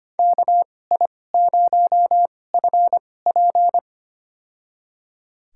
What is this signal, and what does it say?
Morse code, heard as a keyed tone, about 25 wpm, spaced as for 18 wpm, 700 Hz. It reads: KI0FP
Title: USA Call Signs - 5 Characters - 2x Character Spacing - 25wpm